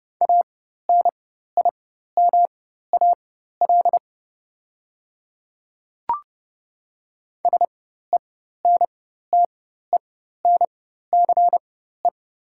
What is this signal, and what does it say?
Morse code, heard as a keyed tone, about 30 wpm, spaced as for 14 wpm, 700 Hz. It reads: ANIMAL SENTENCE
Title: 4x Character Spacing - Top 200 Words - 30wpm